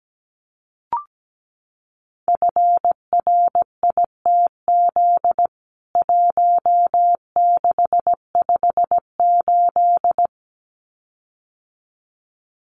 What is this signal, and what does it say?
Morse code, heard as a keyed tone, about 17 wpm, 700 Hz. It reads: FRITZ 1658
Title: CWT Exchange 17wpm